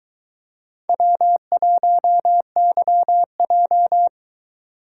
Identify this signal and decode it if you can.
Morse code, heard as a keyed tone, about 22 wpm, 700 Hz. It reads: W1YJ